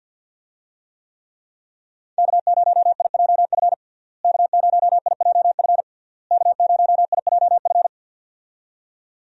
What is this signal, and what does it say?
Morse code, heard as a keyed tone, about 50 wpm, 700 Hz. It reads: K0IJF K0IJF K0IJF
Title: USA Call Signs - 5 Characters - Repeated 3x - 50wpm